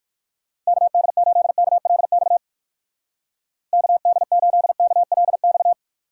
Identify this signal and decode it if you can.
Morse code, heard as a keyed tone, about 68 wpm, spaced as for 58 wpm, 700 Hz. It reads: KD8KLX KD8KLX